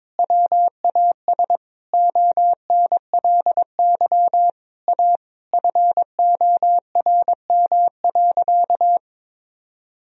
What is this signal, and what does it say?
Morse code, heard as a keyed tone, about 21 wpm, 700 Hz. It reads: WAS ONLY A FORM.
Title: Sentences from Top 1000 Words - Review Part 1 - 22wpm